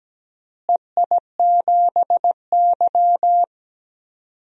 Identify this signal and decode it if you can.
Morse code, heard as a keyed tone, about 17 wpm, 700 Hz. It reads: EI7Y
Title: DXCC-100 Call Signs - 4 Characters - Repeated 2x - 17wpm